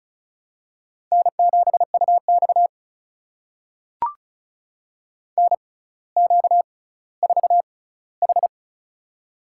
Morse code, heard as a keyed tone, about 36 wpm, spaced as for 18 wpm, 700 Hz. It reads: N7UX NQ4H
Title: USA Call Signs - 4 Characters - 6x Character Spacing - 35wpm